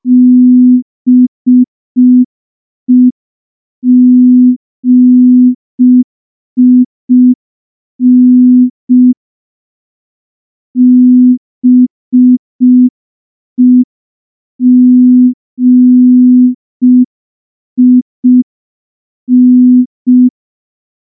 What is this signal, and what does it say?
Morse code, heard as a keyed tone, about 5 wpm, 250 Hz. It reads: BEGIN BEGIN
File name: morse / morse-training-05WPM